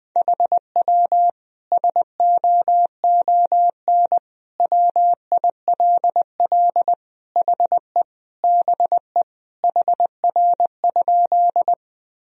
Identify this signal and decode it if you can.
Morse code, heard as a keyed tone, about 21 wpm, 700 Hz. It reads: HW SOON WILL HE BE HR?
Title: Sentences from Top 400 Words and Abbreviations 20wpm